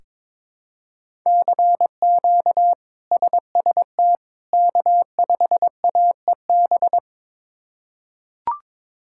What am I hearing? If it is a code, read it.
Morse code, heard as a keyed tone, about 23 wpm, 700 Hz. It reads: CQ SST K5AEB
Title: K1USN SST QSO Practice - 22wpm